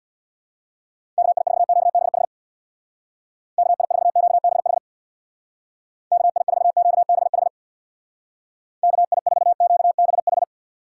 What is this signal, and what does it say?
Morse code, heard as a keyed tone, about 62 wpm, 700 Hz. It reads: KI4QBH KI4QBH KI4QBH KI4QBH
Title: Speed Racing - USA 2x3 Call Signs 50wpm